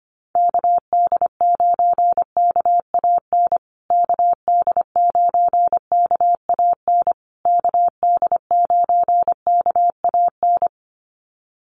Morse code, heard as a keyed tone, about 25 wpm, 700 Hz. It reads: KD9KAN KD9KAN KD9KAN
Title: USA Call Signs Repeated 3x 25wpm